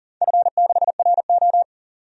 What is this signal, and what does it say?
Morse code, heard as a keyed tone, about 38 wpm, 700 Hz. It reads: F6RO
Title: DXCC-100 Call Signs - 4 Characters - 40wpm